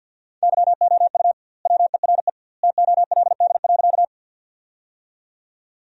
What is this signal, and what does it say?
Morse code, heard as a keyed tone, about 50 wpm, 700 Hz. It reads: YOU WERE TOLD.
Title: 2x Word-Spacing - Sentences from Top 400 Words - 50wpm